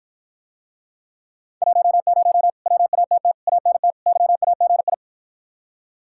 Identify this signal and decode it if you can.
Morse code, heard as a keyed tone, about 53 wpm, 700 Hz. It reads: 10 WATT ANT YAGI